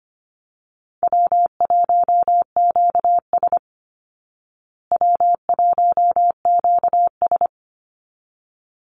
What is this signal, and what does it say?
Morse code, heard as a keyed tone, about 25 wpm, 700 Hz. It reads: W1QS W1QS
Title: Easy Call Signs Repeated 2x - 25wpm